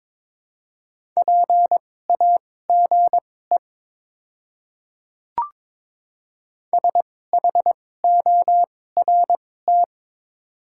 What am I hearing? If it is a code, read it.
Morse code, heard as a keyed tone, about 22 wpm, spaced as for 16 wpm, 700 Hz. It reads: PAGE SHORT
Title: 2x Character Spacing - Top 500 Words - 22wpm